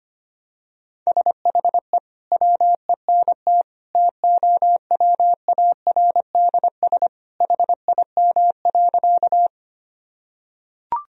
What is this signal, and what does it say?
Morse code, heard as a keyed tone, about 25 wpm, 700 Hz. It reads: SHE WENT TOWARDS HIM.
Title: Sentences from Top 1000 Words - Review Part 2 - 25wpm